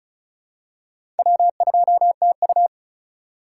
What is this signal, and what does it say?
Morse code, heard as a keyed tone, about 34 wpm, 700 Hz. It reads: W2TU
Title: USA Call Signs - 4 Characters - 4x Character Spacing - 35wpm